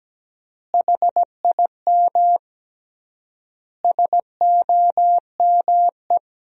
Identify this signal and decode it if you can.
Morse code, heard as a keyed tone, about 17 wpm, 700 Hz. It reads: HIM SOME
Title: 3x Word-Spacing - Sentences from Top 300 Words - 17wpm